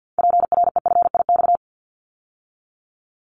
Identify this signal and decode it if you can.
Morse code, heard as a keyed tone, about 50 wpm, 700 Hz. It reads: PREFIX
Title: Callsign Prefixes for Top 50 DXCC - 50wpm